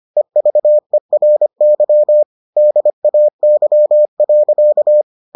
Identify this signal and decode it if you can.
Morse code, heard as a keyed tone, about 25 wpm, 600 Hz. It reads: EVERY DAY.